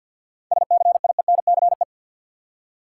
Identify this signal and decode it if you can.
Morse code, heard as a keyed tone, about 53 wpm, 700 Hz. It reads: SCIENCE
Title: Seven Letter Words - 50wpm